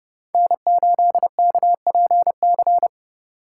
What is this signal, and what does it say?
Morse code, heard as a keyed tone, about 29 wpm, 700 Hz. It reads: N8KPC